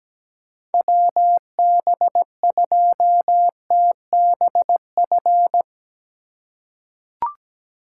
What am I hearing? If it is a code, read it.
Morse code, heard as a keyed tone, about 17 wpm, 700 Hz. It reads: WB2TBF
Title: USA Call Signs - 6 Characters - 2x Character Spacing - 17wpm